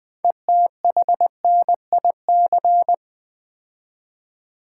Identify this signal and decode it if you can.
Morse code, heard as a keyed tone, about 20 wpm, 700 Hz. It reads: ETHNIC
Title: 6 Letter Words - First 3 Letters Given - 20wpm